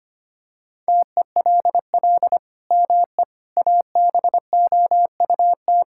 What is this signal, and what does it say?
Morse code, heard as a keyed tone, about 25 wpm, 700 Hz. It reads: TELL ME ABOUT
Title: Sentences from Top 1000 Words - Review Part 6 - 25wpm